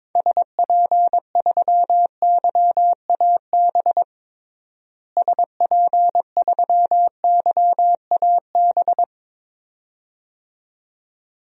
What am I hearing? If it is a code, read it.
Morse code, heard as a keyed tone, about 22 wpm, 700 Hz. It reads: SP3YAB SP3YAB
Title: DXCC-100 Call Signs - 6 Characters - Repeated 2x - 22wpm